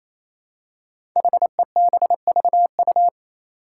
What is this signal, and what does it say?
Morse code, heard as a keyed tone, about 28 wpm, 700 Hz. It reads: HEBVU
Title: Five Challenging Letter-Number Combinations 28wpm